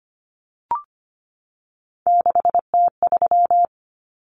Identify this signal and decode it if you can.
Morse code, heard as a keyed tone, about 25 wpm, 700 Hz. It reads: 6T3
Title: Common Cut Numbers - 3-Digit 25wpm